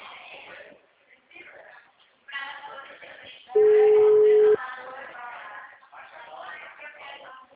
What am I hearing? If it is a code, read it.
Ringback tone (Europe, 425 Hz), with speech.